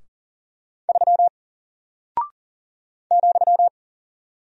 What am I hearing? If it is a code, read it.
Morse code, heard as a keyed tone, about 39 wpm, 700 Hz. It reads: 3 ,
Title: Single Character (Letter:Number:Punctuation:Prosign) 40wpm